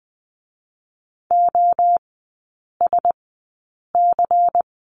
Morse code, heard as a keyed tone, about 20 wpm, spaced as for 8 wpm, 700 Hz. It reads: OSC
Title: 093 Lesson 16e - Letter c - Three Characters 20wpm